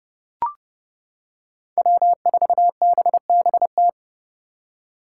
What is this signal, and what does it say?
Morse code, heard as a keyed tone, about 30 wpm, 700 Hz. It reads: W4BBT
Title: K1USN SST QSO Practice - 30wpm